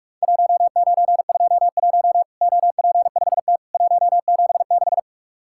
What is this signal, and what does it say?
Morse code, heard as a keyed tone, about 44 wpm, 700 Hz. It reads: 1921 OP5T 176
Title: Worked All Europe DX Contest - QTC Exchange - 45wpm